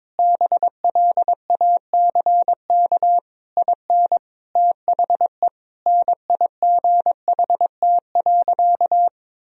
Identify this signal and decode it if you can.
Morse code, heard as a keyed tone, about 22 wpm, 700 Hz. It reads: BLACK IN THE NIGHT.
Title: Sentences from Top 1000 Words - Review Part 4 - 22wpm